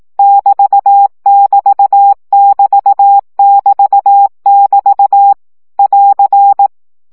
Morse code, heard as a keyed tone, about 18 wpm, 800 Hz. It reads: <BT><BT><BT><BT><BT> <AR>
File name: CW gyakorlas egyenlo 15 baud